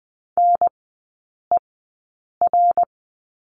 Morse code, heard as a keyed tone, about 20 wpm, spaced as for 8 wpm, 700 Hz. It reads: NER
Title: 081 Lesson 14e - Number 2 - Three Characters 20wpm